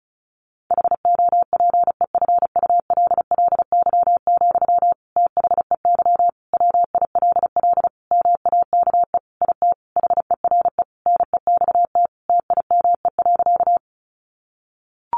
Morse code, heard as a keyed tone, about 34 wpm, 700 Hz. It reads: HOPEFULLY, THEY WILL MAKE IT HERE NEXT TIME.
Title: Sentences from Top 1500 Words 35wpm